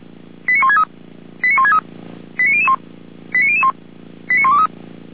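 Five-tone selective call (ZVEI), heard as quick runs of five tones, tones of about 70 ms. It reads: 89163 89163 89001 89001 89123
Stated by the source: ZVEI five-tone call 89130 received by SDR, channel 455.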